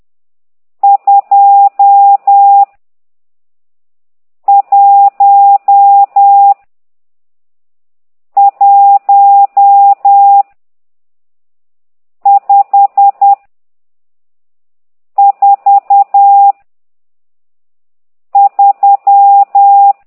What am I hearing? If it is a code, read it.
Morse code, heard as a keyed tone, about 10 wpm, 800 Hz. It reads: 2 1 1 5 4 3
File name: ct2idw / exerc12